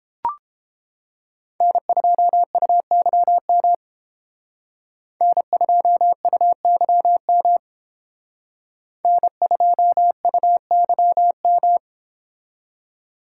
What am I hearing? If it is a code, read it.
Morse code, heard as a keyed tone, about 32 wpm, 700 Hz. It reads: N2UYM N2UYM N2UYM